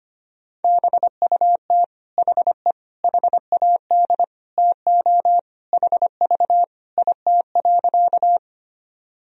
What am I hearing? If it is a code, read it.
Morse code, heard as a keyed tone, about 25 wpm, 700 Hz. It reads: BUT HE HAD TO HV IT.
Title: Sentences from Top 100 Words and Abbreviations 25wpm